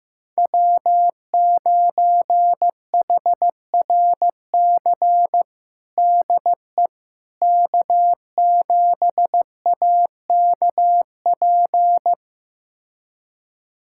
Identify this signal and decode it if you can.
Morse code, heard as a keyed tone, about 15 wpm, 700 Hz. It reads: W9HRC DE K7AKP
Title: USA Callsign DE Callsign - 5 Characters - 15wpm